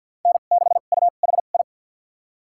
Morse code, heard as a keyed tone, about 47 wpm, spaced as for 35 wpm, 700 Hz. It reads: N6UHI
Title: USA Call Signs - 5 Characters - 2x Character Spacing - 50wpm